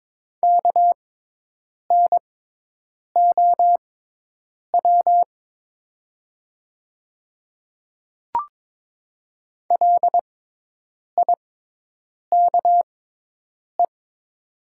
Morse code, heard as a keyed tone, about 22 wpm, spaced as for 8 wpm, 700 Hz. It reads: KNOW LIKE
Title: 6x Character Spacing - Top 300 Words - 22wpm